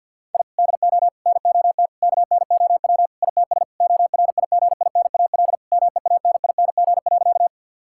Morse code, heard as a keyed tone, about 50 wpm, 700 Hz. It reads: I DO NOT KNOW ITS ORIGINAL MEANING.
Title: Sentences from Top 1000 Words - Review Part 6 - 50wpm